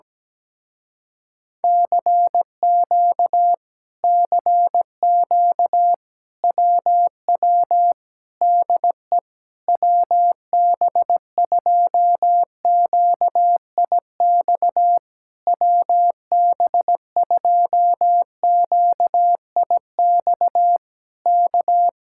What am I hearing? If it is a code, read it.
Morse code, heard as a keyed tone, about 17 wpm, 700 Hz. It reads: CQ CQ WW DE WB2QIX WB2QIX K
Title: Speed-Racing Common QSO Phrases - 17wpm